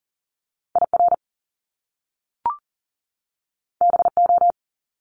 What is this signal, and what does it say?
Morse code, heard as a keyed tone, about 40 wpm, 700 Hz. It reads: IR BO
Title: Top Two Letter Combinations in Words 40wpm